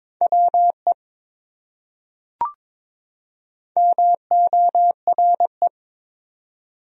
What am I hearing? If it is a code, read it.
Morse code, heard as a keyed tone, about 22 wpm, 700 Hz. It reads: WE MORE
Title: Top 100 Words 22wpm